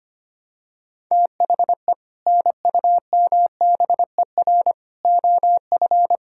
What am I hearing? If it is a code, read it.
Morse code, heard as a keyed tone, about 25 wpm, 700 Hz. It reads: THE NUMBER OF
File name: practice / 25wpm / Sentences with 3-Word Phrases - 25wpm